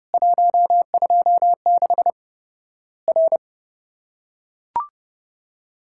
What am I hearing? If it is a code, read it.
Morse code, heard as a keyed tone, about 30 wpm, 700 Hz. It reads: 126 R
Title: Worked All Europe DX Contest - QTC Exchange - 30wpm